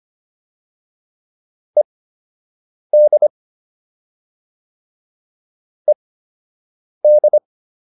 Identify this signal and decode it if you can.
Morse code, heard as a keyed tone, about 25 wpm, spaced as for 7 wpm, 600 Hz. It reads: ED ED